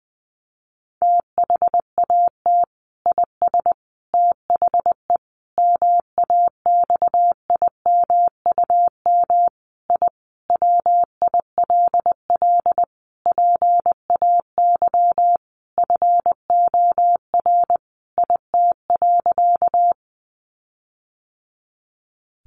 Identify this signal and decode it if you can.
Morse code, heard as a keyed tone, about 20 wpm, 700 Hz. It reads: THAT IS THE MAXIMUM I WILL PAY FOR IT.
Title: Sentences from Top 1300 Words 20wpm